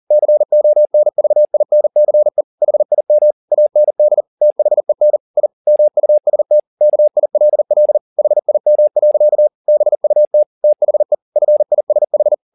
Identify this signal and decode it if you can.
Morse code, heard as a keyed tone, about 40 wpm, 600 Hz. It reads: CONVINCE HIM AND THEN I MUST KILL HIM. BUT THE FISH
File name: the old man and the sea 2 40wpm